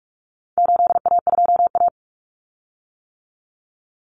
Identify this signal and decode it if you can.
Morse code, heard as a keyed tone, about 44 wpm, 700 Hz. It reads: 8A2A
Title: Common Cut Numbers - 4-Digit 45wpm